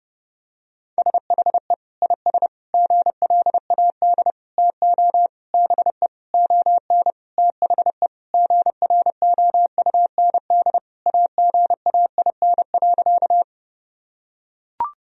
Morse code, heard as a keyed tone, about 30 wpm, 700 Hz. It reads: SHE IS GLAD TO BE ON THE GROUND AGAIN.
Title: Sentences from Top 1000 Words - Review Part 2 - 30wpm